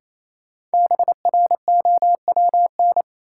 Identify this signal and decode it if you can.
Morse code, heard as a keyed tone, about 28 wpm, 700 Hz. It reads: BROWN
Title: Sentences from Top 1200 Words - New Word Drill - 28wpm